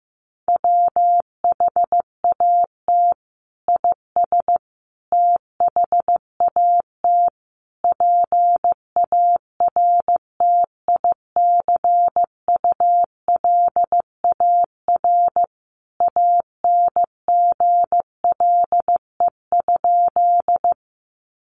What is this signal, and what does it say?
Morse code, heard as a keyed tone, about 15 wpm, 700 Hz. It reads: WHAT IS THAT PARTICULAR ANGLE?
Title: Sentences from Top 1500 Words 15wpm